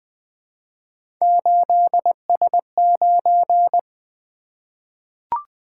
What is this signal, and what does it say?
Morse code, heard as a keyed tone, about 20 wpm, 700 Hz. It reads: 8S9
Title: Three Letter-Number Combinations - 20wpm